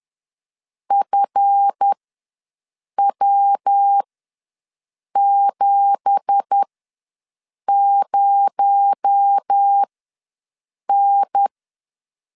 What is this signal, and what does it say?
Morse code, heard as a keyed tone, about 10 wpm, spaced as for 6 wpm, 800 Hz. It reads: FW70N
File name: LEZIONI / 10 SIGNS / 10A SLASH ws 10wpm 4121